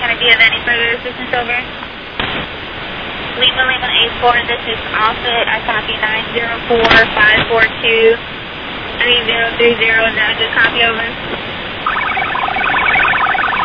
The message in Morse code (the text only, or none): none